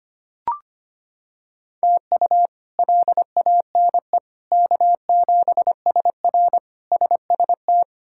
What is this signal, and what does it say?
Morse code, heard as a keyed tone, about 25 wpm, 700 Hz. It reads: TU LANE K7SR SST
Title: K1USN SST QSO Practice - 25wpm